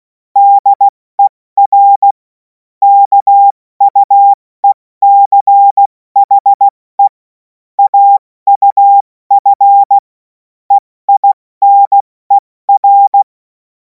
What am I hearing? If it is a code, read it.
Morse code, heard as a keyed tone, about 16 wpm, spaced as for 14 wpm, 800 Hz. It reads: DER KUECHE AUF EINER
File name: Morsekurs / Witze / Witz 11. August 2026 70 F80